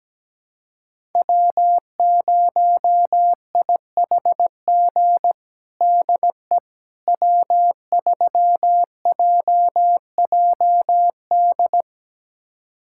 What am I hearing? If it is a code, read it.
Morse code, heard as a keyed tone, about 17 wpm, 700 Hz. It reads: W0IHG DE W3JJD
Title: USA Callsign DE Callsign - 5 Characters - 17wpm